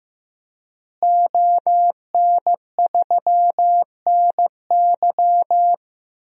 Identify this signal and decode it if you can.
Morse code, heard as a keyed tone, about 15 wpm, 700 Hz. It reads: ON3NY
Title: DXCC-100 Call Signs - 5 Characters - Repeated 3x - 15wpm